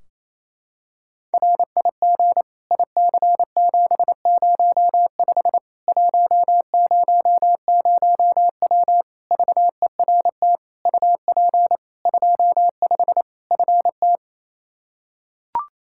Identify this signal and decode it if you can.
Morse code, heard as a keyed tone, about 28 wpm, 700 Hz. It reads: RIG IC705 100W VERT UP 25 FT
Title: K1USN SST QSO Practice 28wpm